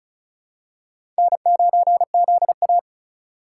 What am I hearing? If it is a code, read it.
Morse code, heard as a keyed tone, about 34 wpm, 700 Hz. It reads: N9ZA
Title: USA Call Signs - 4 Characters - 4x Character Spacing - 35wpm